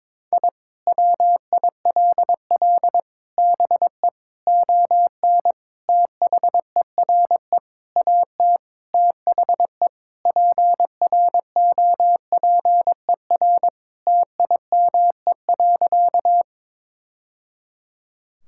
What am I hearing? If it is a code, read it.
Morse code, heard as a keyed tone, about 21 wpm, 700 Hz. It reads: I WILL BE ON THERE AT THE PROPER TIME.
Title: Sentences from Top 1000 Words - Review Part 5 - 22wpm